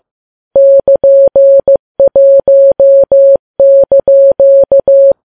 Morse code, heard as a keyed tone, about 15 wpm, 550 Hz. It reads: <KN>1)
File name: Files Web tls / Morse on9fv / morse15kps